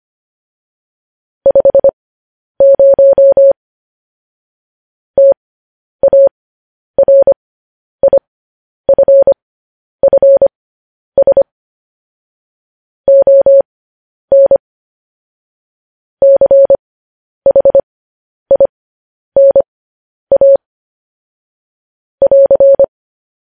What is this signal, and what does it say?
Morse code, heard as a keyed tone, about 25 wpm, spaced as for 10 wpm, 550 Hz. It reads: 50 TARIFFS ON CHINA <AR>
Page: news headlines in morse code at 10 wpm